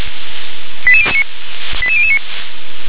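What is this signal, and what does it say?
Five-tone selective call (ZVEI), heard as quick runs of five tones, tones of about 70 ms. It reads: 88009 88009
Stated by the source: ZVEI five-tone selective call received by SDR.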